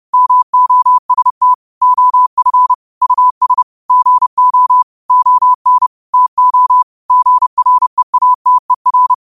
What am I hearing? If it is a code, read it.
Morse code, heard as a keyed tone, about 30 wpm, 1000 Hz. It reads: MOST OF US GO ON TO GREATER
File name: qotdfiles / 30wpm